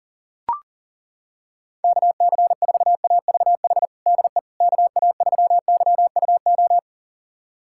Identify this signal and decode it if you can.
Morse code, heard as a keyed tone, about 40 wpm, 700 Hz. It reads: KC4AVH DE KA3YUO
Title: USA Callsign DE Callsign - 6 Characters - 40wpm